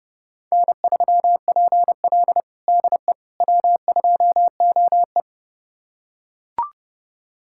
Morse code, heard as a keyed tone, about 30 wpm, 700 Hz. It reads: N3PL DE W2OE
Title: USA Callsign DE Callsign - 4 Characters - 30wpm